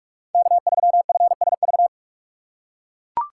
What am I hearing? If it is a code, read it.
Morse code, heard as a keyed tone, about 42 wpm, 700 Hz. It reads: K3FSV
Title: USA Call Signs - 5 Characters - 6x Character Spacing - 45wpm